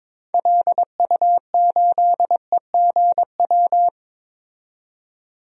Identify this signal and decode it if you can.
Morse code, heard as a keyed tone, about 22 wpm, 700 Hz. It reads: LU8EGW
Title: DXCC-100 Call Signs - 6 Characters - 22wpm